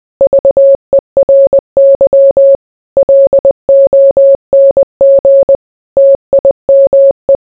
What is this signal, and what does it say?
Morse code, heard as a keyed tone, about 20 wpm, 550 Hz. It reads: VERY LONG TIME